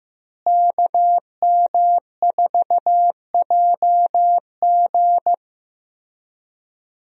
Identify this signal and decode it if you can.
Morse code, heard as a keyed tone, about 15 wpm, 700 Hz. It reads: KM4JG
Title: USA Call Signs - 5 Characters - Repeated 2x - 15wpm